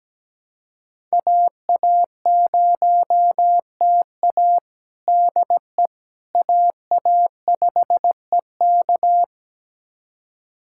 Morse code, heard as a keyed tone, about 17 wpm, 700 Hz. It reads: AA0TA DE AA5EK
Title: USA Callsign DE Callsign - 5 Characters - 17wpm